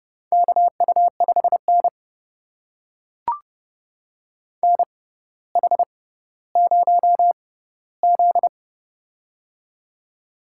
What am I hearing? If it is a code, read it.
Morse code, heard as a keyed tone, about 30 wpm, spaced as for 15 wpm, 700 Hz. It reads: KU5N NH0Z
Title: USA Call Signs - 4 Characters - 6x Character Spacing - 30wpm